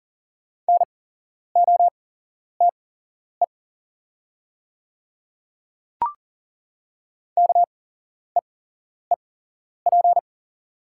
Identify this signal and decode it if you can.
Morse code, heard as a keyed tone, about 37 wpm, spaced as for 11 wpm, 690 Hz. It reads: NOTE KEEP
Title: 8x Character Spacing - Top 500 Words - 40wpm